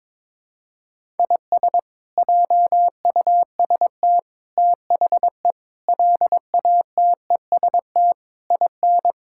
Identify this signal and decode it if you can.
Morse code, heard as a keyed tone, about 21 wpm, 700 Hz. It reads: IS JUST THE LATEST IN
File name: practice / 22wpm / Sets of 8 Words - 22wpm